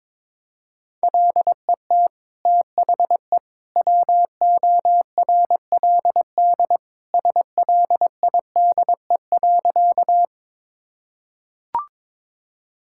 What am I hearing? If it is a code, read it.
Morse code, heard as a keyed tone, about 22 wpm, 700 Hz. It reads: LET THE WORLD SLIDE.
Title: Sentences from Top 1600 Words 22wpm